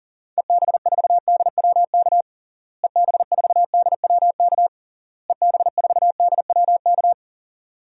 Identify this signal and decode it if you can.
Morse code, heard as a keyed tone, about 40 wpm, 700 Hz. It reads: EB4DWK EB4DWK EB4DWK
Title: DXCC-100 Call Signs - 6 Characters - Repeated 3x - 40wpm